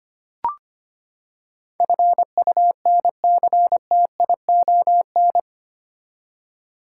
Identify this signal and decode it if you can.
Morse code, heard as a keyed tone, about 25 wpm, 700 Hz. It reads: FUNCTION
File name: Eight Letter Words - 25wpm